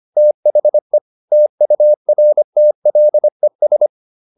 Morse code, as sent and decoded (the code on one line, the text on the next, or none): - .... / - ..- .-. - .-.. ...
THE TURTLES